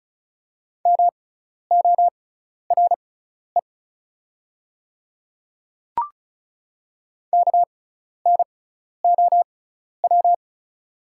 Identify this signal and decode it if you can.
Morse code, heard as a keyed tone, about 36 wpm, spaced as for 12 wpm, 700 Hz. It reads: MORE KNOW
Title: 6x Character Spacing - Top 200 Words - 35wpm